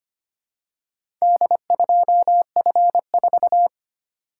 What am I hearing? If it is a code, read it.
Morse code, heard as a keyed tone, about 25 wpm, 700 Hz. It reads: D2F4
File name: practice / 25wpm / Four Letter-Number Combinations - 25wpm